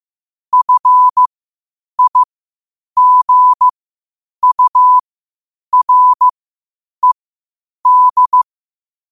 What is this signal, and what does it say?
Morse code, heard as a keyed tone, about 15 wpm, spaced as for 8 wpm, 1000 Hz. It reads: FIGURED
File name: qotdfiles / 8wpm